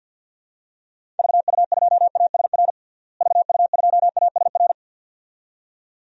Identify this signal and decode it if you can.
Morse code, heard as a keyed tone, about 47 wpm, 700 Hz. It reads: VU2ASR VU2ASR
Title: DXCC-100 Call Signs - 6 Characters - Repeated 2x - 50wpm